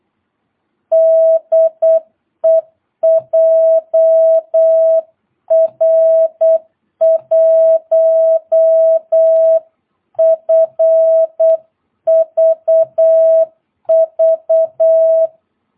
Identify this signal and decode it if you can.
Morse code, heard as a keyed tone, about 8 wpm, 650 Hz. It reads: DE JR1FVV
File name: jh8cbh / 2022cw